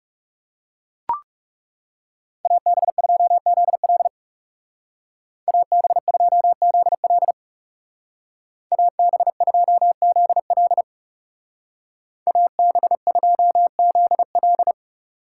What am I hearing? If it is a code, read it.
Morse code, heard as a keyed tone, about 37 wpm, 700 Hz. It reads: AB2ZL AB2ZL AB2ZL AB2ZL
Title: Speed Racing - USA 2x2 Call Signs 30wpm